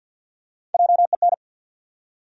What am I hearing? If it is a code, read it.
Morse code, heard as a keyed tone, about 50 wpm, 700 Hz. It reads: JEN
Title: USA Common Names - Max 3 Letters - 50wpm